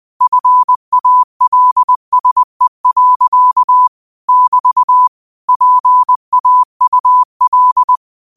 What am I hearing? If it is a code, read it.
Morse code, heard as a keyed tone, about 20 wpm, 1000 Hz. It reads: FALSE. <BT> PAUL